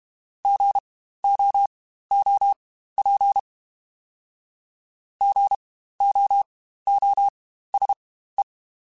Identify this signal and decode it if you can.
Morse code, heard as a keyed tone, about 32 wpm, spaced as for 15 wpm, 800 Hz. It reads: GOOP GOOSE
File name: Spaced Alphas / Spaced Alphas mp3s / Gs sp @30